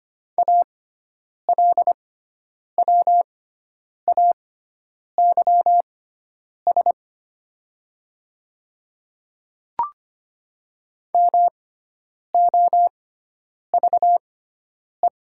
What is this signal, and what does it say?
Morse code, heard as a keyed tone, about 25 wpm, spaced as for 9 wpm, 700 Hz. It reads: ALWAYS MOVE
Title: 6x Character Spacing - Top 300 Words - 25wpm